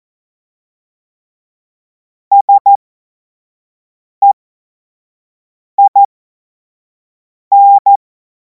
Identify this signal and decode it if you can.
Morse code, heard as a keyed tone, about 14 wpm, spaced as for 5 wpm, 800 Hz. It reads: SEIN
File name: Morsekurs / Witze / Witz 31. März 2026 25 F70